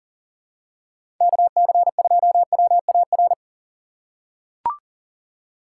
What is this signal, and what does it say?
Morse code, heard as a keyed tone, about 39 wpm, 700 Hz. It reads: KC2WAR
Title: USA Call Signs - 6 Characters - 4x Character Spacing - 40wpm